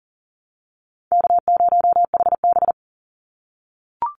Morse code, heard as a keyed tone, about 41 wpm, 700 Hz. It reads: K0HB
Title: Easy Call Signs - 40wpm